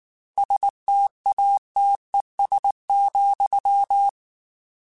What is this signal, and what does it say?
Morse code, heard as a keyed tone, about 19 wpm, 800 Hz. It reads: STATES,
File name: Technical Subjects / Technical Subjects-mp3s / AND Gate @18